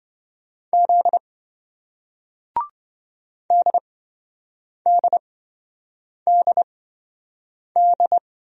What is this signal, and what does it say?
Morse code, heard as a keyed tone, about 26 wpm, spaced as for 7 wpm, 700 Hz. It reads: Z DDDD